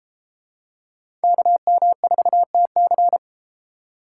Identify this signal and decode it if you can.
Morse code, heard as a keyed tone, about 33 wpm, 700 Hz. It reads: KM4TC